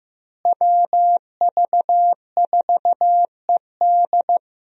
Morse code, heard as a keyed tone, about 15 wpm, 700 Hz. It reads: WV4ED
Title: USA Call Signs - 5 Characters - Repeated 2x - 15wpm